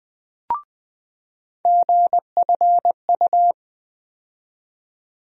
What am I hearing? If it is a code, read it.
Morse code, heard as a keyed tone, about 20 wpm, 700 Hz. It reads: GFU
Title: USA Call Sign Postfixes - 3 Letters 20wpm